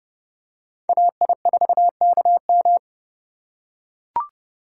Morse code, heard as a keyed tone, about 32 wpm, 700 Hz. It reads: AI4KM
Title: USA Call Signs - 5 Characters - 3x Character Spacing - 30wpm